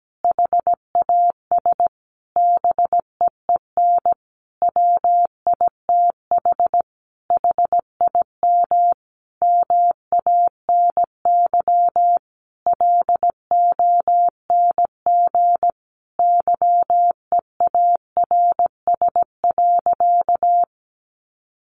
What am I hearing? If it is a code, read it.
Morse code, heard as a keyed tone, about 17 wpm, 700 Hz. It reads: HAS BEEN WITH HIM MANY LONG YEARS.
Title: Sentences from Top 200 Words 17wpm